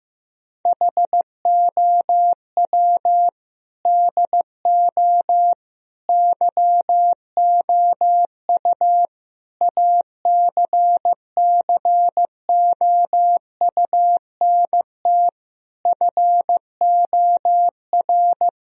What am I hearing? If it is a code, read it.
Morse code, heard as a keyed tone, about 15 wpm, 700 Hz. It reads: HOW DO YOU ACCOUNT FOR
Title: Sentences from Top 1000 Words - Review Part 3 - 15wpm